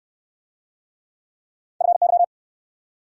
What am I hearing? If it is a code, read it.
Morse code, heard as a keyed tone, about 64 wpm, 700 Hz. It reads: FY